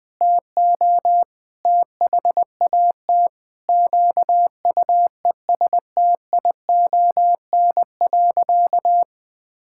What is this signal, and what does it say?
Morse code, heard as a keyed tone, about 21 wpm, 700 Hz. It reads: TO THAT QUESTION.
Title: Sentences with 7-Word Phrases - 20wpm